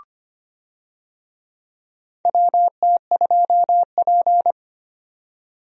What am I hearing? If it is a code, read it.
Morse code, heard as a keyed tone, about 25 wpm, 700 Hz. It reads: WT2P
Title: USA Call Signs - 4 Characters - 25wpm